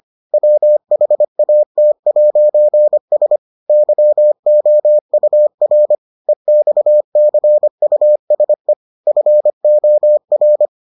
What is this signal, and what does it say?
Morse code, heard as a keyed tone, about 25 wpm, 600 Hz. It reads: WHAT'S YOUR EXCUSE FOR